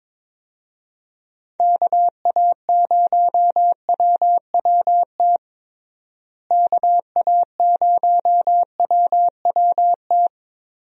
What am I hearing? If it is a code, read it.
Morse code, heard as a keyed tone, about 21 wpm, 700 Hz. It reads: KA0WWT KA0WWT
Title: USA Call Signs - 6 Characters - Repeated 3x - 22wpm